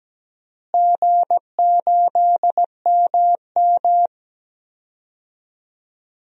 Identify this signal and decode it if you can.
Morse code, heard as a keyed tone, about 17 wpm, 700 Hz. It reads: G8MM
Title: DXCC-100 Call Signs - 4 Characters - Repeated 3x - 17wpm